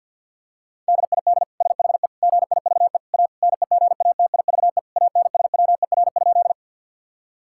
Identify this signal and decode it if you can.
Morse code, heard as a keyed tone, about 50 wpm, 700 Hz. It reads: DID SHE GIVE A NEGATIVE ANSWER?
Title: Sentences from Top 1000 Words - Review Part 6 - 50wpm